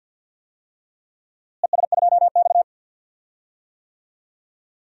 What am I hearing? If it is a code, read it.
Morse code, heard as a keyed tone, about 51 wpm, 700 Hz. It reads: ES2X
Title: DXCC-100 Call Signs - 4 Characters - Repeated 2x - 50wpm